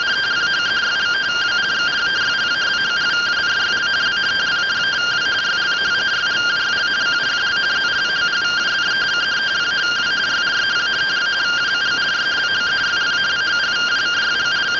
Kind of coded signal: two-tone teleprinter signal

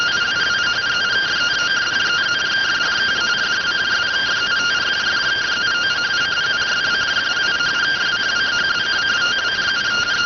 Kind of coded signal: two-tone teleprinter signal

